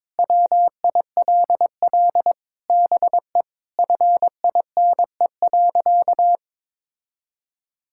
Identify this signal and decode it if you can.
Morse code, heard as a keyed tone, about 22 wpm, 700 Hz. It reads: WILL BE FINE.
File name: Sentences with 7-Word Phrases - 22wpm